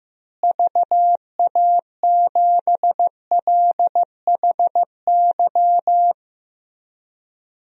Morse code, heard as a keyed tone, about 16 wpm, 700 Hz. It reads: VA7LHY
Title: DXCC-100 Call Signs - 6 Characters - Repeated 3x - 15wpm